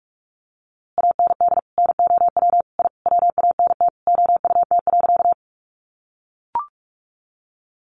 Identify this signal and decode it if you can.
Morse code, heard as a keyed tone, about 44 wpm, 700 Hz. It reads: AND NOW I WANT OUT.
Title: A Sound Of Thunder 45wpm